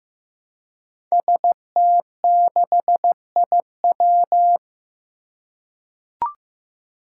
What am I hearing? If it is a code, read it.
Morse code, heard as a keyed tone, about 15 wpm, 700 Hz. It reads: ST6IW
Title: Five Challenging Letter-Number Combinations 15wpm